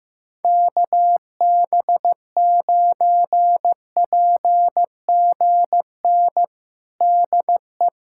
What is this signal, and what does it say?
Morse code, heard as a keyed tone, about 15 wpm, 700 Hz. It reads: KB9PGN DE
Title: USA Callsign DE Callsign - 6 Characters - 15wpm